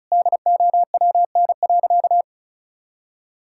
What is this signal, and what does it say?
Morse code, heard as a keyed tone, about 36 wpm, 700 Hz. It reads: DOWN.
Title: Sentences from Top 1600 Words 35wpm